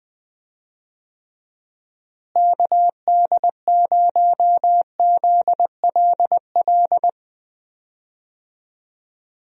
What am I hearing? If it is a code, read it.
Morse code, heard as a keyed tone, about 20 wpm, 700 Hz. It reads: KD0ZLL
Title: USA Call Signs - 6 Characters - Repeated 2x - 20wpm